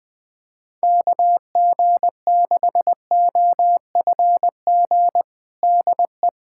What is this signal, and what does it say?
Morse code, heard as a keyed tone, about 21 wpm, 700 Hz. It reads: KG6OFG DE
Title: USA Callsign DE Callsign 20wpm